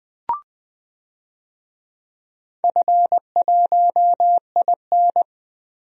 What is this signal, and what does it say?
Morse code, heard as a keyed tone, about 20 wpm, 700 Hz. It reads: F1IN
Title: DXCC-100 Call Signs - 4 Characters - 20wpm